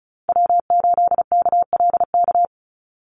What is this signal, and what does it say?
Morse code, heard as a keyed tone, about 34 wpm, 700 Hz. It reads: W8KLK
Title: USA Call Signs 35wpm